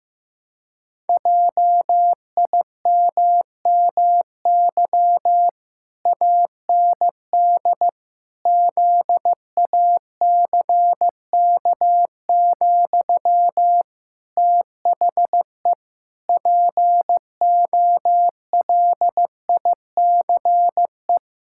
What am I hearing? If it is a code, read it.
Morse code, heard as a keyed tone, about 15 wpm, 700 Hz. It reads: JIMMY AND ZACK, THE POLICE
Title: Pangram Challenge v2 - 15wpm